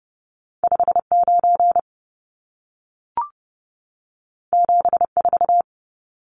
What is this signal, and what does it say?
Morse code, heard as a keyed tone, about 30 wpm, 700 Hz. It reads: 59 74
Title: Numbers - 2-Digit 30wpm